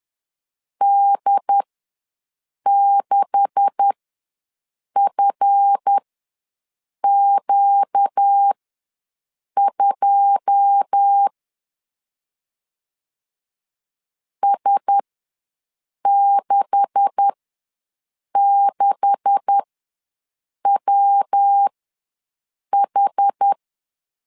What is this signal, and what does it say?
Morse code, heard as a keyed tone, about 10 wpm, spaced as for 6 wpm, 800 Hz. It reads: D6FQ2 S66WH
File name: LEZIONI / 9 286 ws 10wpm 4032